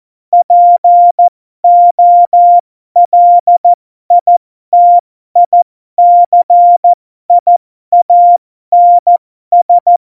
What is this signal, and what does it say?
Morse code, heard as a keyed tone, about 14 wpm, spaced as for 12 wpm, 700 Hz. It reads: POLITICIANS
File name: Morsekurs / Witze